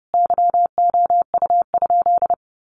Morse code, heard as a keyed tone, about 30 wpm, 700 Hz. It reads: YOU?
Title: Sentences from Top 1300 Words 30wpm